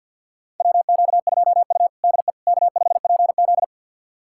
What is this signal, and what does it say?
Morse code, heard as a keyed tone, about 50 wpm, 700 Hz. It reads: WQ2U DE K5PZ
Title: USA Callsign DE Callsign - 4 Characters - 50wpm